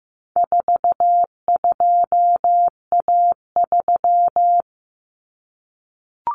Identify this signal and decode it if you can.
Morse code, heard as a keyed tone, about 15 wpm, 700 Hz. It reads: 42A3
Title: Common Cut Numbers - 4-Digit 15wpm